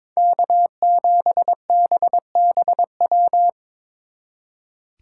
Morse code, heard as a keyed tone, about 26 wpm, 700 Hz. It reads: K7BBW
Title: Speed Racing - USA 1x3 Call Signs 22wpm